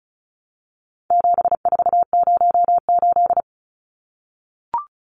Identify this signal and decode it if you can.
Morse code, heard as a keyed tone, about 34 wpm, 700 Hz. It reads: 7408
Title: Numbers - 4-Digit 35wpm